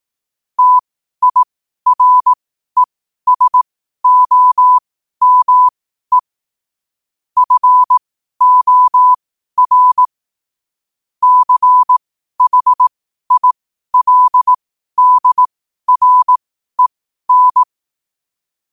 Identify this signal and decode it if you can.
Morse code, heard as a keyed tone, about 18 wpm, spaced as for 12 wpm, 1000 Hz. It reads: TIRESOME FOR CHILDREN
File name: qotdfiles / 12wpm